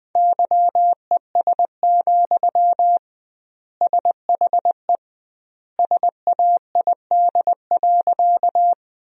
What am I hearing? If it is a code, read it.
Morse code, heard as a keyed tone, about 20 wpm, 700 Hz. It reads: YES, SHE SAID.
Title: 2x Word-Spacing - Sentences from Top 500 Words - 20wpm